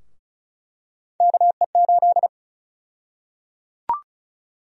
Morse code, heard as a keyed tone, about 36 wpm, 700 Hz. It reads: KE8
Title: USA Call Sign Prefixes - Repeated 2x - 35wpm